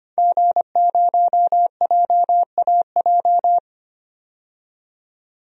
Morse code, heard as a keyed tone, about 25 wpm, 700 Hz. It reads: G0JAJ
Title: DXCC-100 Call Signs - 5 Characters - 25wpm